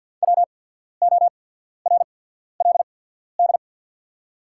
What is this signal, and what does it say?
Morse code, heard as a keyed tone, about 48 wpm, spaced as for 14 wpm, 700 Hz. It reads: WORLD
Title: 8x Character Spacing - Top 500 Words - 50wpm